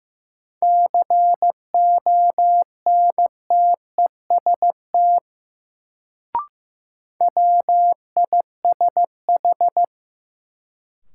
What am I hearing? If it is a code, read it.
Morse code, heard as a keyed tone, about 15 wpm, 700 Hz. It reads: CONTEST WISH
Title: Sentences from Top 1600 Words - New Word Drill - 15wpm